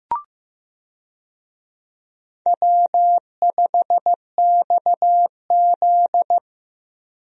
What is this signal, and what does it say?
Morse code, heard as a keyed tone, about 15 wpm, 700 Hz. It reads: W5XZ
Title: USA Call Signs - 4 Characters - Repeated 2x - 15wpm